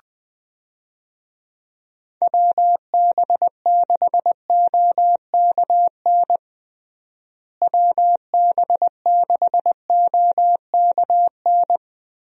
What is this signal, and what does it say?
Morse code, heard as a keyed tone, about 20 wpm, 700 Hz. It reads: WB6OKN WB6OKN
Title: USA Call Signs - 6 Characters - Repeated 3x - 20wpm